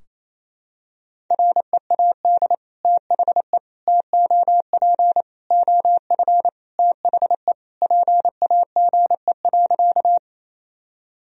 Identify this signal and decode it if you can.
Morse code, heard as a keyed tone, about 28 wpm, 700 Hz. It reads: READ THE TOP OF THE PAGE.
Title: Sentences from Top 1000 Words - Review Part 2 - 28wpm